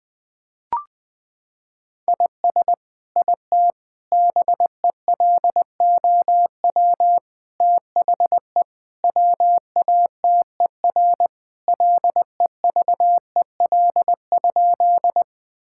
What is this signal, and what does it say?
Morse code, heard as a keyed tone, about 20 wpm, 700 Hz. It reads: IS IT BELOW THE WATER LEVEL?
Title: Sentences from Top 1000 Words - Review Part 5 - 20wpm